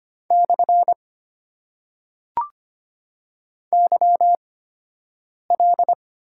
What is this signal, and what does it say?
Morse code, heard as a keyed tone, about 25 wpm, spaced as for 7 wpm, 700 Hz. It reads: / YL
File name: Top 90 QSO Elements - 8x Character Spacing - 25wpm